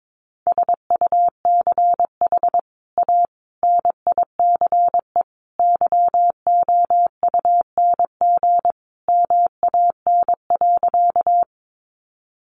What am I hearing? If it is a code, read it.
Morse code, heard as a keyed tone, about 22 wpm, 700 Hz. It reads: SUCH A NICE YOUNG MAN.
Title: Sentences from Top 1100 Words 22wpm